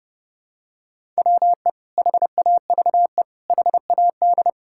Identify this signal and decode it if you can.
Morse code, heard as a keyed tone, about 30 wpm, 700 Hz. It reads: WE HAVE HAD
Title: Sentences from Top 1000 Words - Review Part 5 - 30wpm